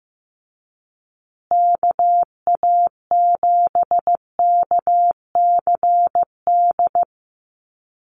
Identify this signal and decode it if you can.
Morse code, heard as a keyed tone, about 15 wpm, 700 Hz. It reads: KA7KCD
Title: USA Call Signs Repeated 3x 15wpm